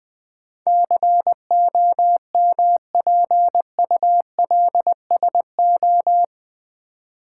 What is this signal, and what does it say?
Morse code, heard as a keyed tone, about 20 wpm, 700 Hz. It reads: COMPULSO
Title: Build A Word - 4 Letters At A Time - 10-Letter Words - 20wpm